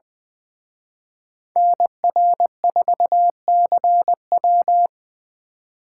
Morse code, heard as a keyed tone, about 23 wpm, 700 Hz. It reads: NR4CW
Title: Speed Racing - USA 2x2 Call Signs 20wpm